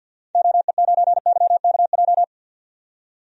Morse code, heard as a keyed tone, about 50 wpm, 700 Hz. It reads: OE9YKJ
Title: DXCC-100 Call Signs - 6 Characters - 50wpm